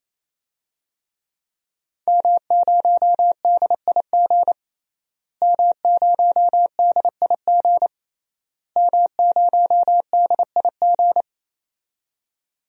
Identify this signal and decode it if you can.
Morse code, heard as a keyed tone, about 28 wpm, 700 Hz. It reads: M0DIG M0DIG M0DIG
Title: DXCC-100 Call Signs - 5 Characters - Repeated 3x - 28wpm